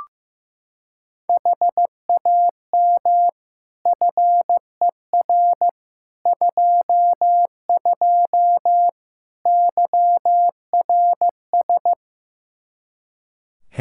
Morse code, heard as a keyed tone, about 15 wpm, 700 Hz. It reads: HAM FER 22 YRS
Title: Practice QSO - Exchange 3 - More Info - 15wpm